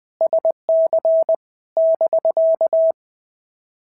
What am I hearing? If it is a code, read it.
Morse code, heard as a keyed tone, about 20 wpm, 650 Hz. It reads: SC <BK>
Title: POTA QSO Practice - 20wpm